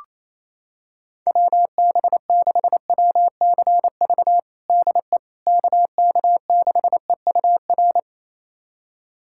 Morse code, heard as a keyed tone, about 27 wpm, 700 Hz. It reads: WB6WCV DE KK6EUR
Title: USA Callsign DE Callsign - 6 Characters - 28wpm